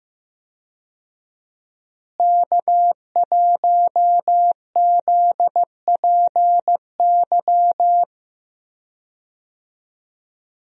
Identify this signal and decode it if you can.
Morse code, heard as a keyed tone, about 15 wpm, 700 Hz. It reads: K1ZPY
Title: USA Call Signs - 5 Characters - Repeated 3x - 15wpm